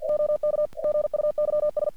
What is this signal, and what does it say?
Morse code, heard as a keyed tone, about 48 wpm, 600 Hz. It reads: OK GUYS
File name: archived / CRSnet